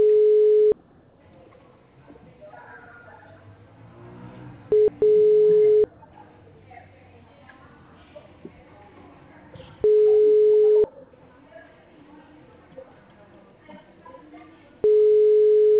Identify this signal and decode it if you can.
Ringback tone (Europe, 425 Hz), with speech.